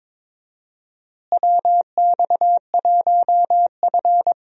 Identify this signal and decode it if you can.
Morse code, heard as a keyed tone, about 22 wpm, 700 Hz. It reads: WX1F
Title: USA Call Signs - 4 Characters - Repeated 3x - 22wpm